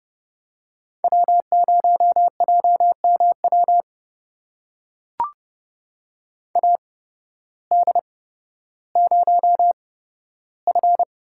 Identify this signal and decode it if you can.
Morse code, heard as a keyed tone, about 30 wpm, spaced as for 14 wpm, 700 Hz. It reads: W0JMW AD0F
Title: USA Call Signs - 5 Characters - 8x Character Spacing - 30wpm